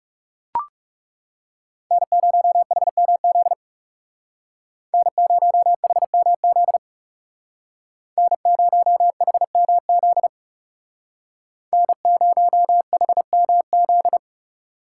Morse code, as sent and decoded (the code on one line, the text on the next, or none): -. ----- .... -- --.. / -. ----- .... -- --.. / -. ----- .... -- --.. / -. ----- .... -- --..
N0HMZ N0HMZ N0HMZ N0HMZ